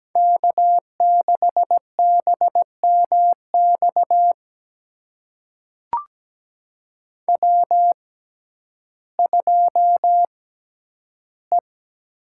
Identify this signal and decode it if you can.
Morse code, heard as a keyed tone, about 17 wpm, spaced as for 10 wpm, 700 Hz. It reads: K6BMX W2E
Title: USA Call Signs - 5 Characters - 6x Character Spacing - 17wpm